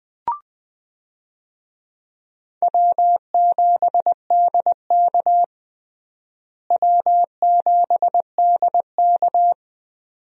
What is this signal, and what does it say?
Morse code, heard as a keyed tone, about 20 wpm, 700 Hz. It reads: W7DK W7DK
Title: USA Call Signs - 4 Characters - Repeated 2x - 20wpm